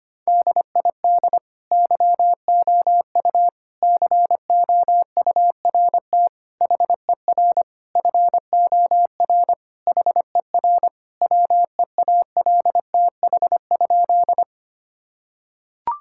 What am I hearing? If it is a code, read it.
Morse code, heard as a keyed tone, about 25 wpm, 700 Hz. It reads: DID YOU COURT HER FOR HER WEALTH?
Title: Sentences from Top 1600 Words 25wpm